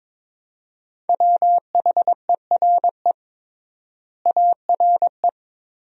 Morse code, heard as a keyed tone, about 22 wpm, 700 Hz. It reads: WHERE ARE
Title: 3x Word-Spacing - Sentences from Top 200 Words - 22wpm